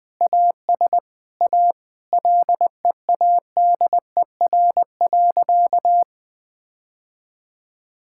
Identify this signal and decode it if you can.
Morse code, heard as a keyed tone, about 20 wpm, 700 Hz. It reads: AS A LEADER.
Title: Sentences with 2-Word Phrases - 20wpm